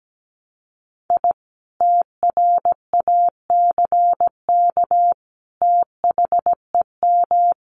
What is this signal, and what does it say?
Morse code, heard as a keyed tone, about 17 wpm, 700 Hz. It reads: I TRACK THEM
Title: A Sound Of Thunder 17wpm